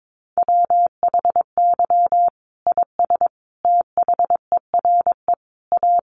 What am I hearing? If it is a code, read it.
Morse code, heard as a keyed tone, about 22 wpm, 700 Hz. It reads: WHY IS THERE A
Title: Sentences from Top 1200 Words 22wpm